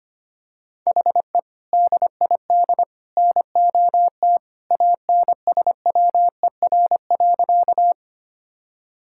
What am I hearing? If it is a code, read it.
Morse code, heard as a keyed tone, about 25 wpm, 700 Hz. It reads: HE DID NOT ANSWER.
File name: practice / 25wpm / Sentences with 3-Word Phrases - 25wpm